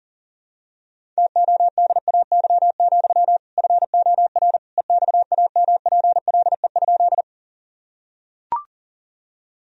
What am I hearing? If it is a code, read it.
Morse code, heard as a keyed tone, about 40 wpm, 700 Hz. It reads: TODAY, FOR EXAMPLE?
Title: Sentences from Top 1000 Words - Review Part 1 - 40wpm